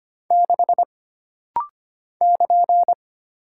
Morse code, heard as a keyed tone, about 25 wpm, 700 Hz. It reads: 6 <KN>
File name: Single Character (Letter:Number:Punctuation:Prosign) - Rapid-Fire - 25wpm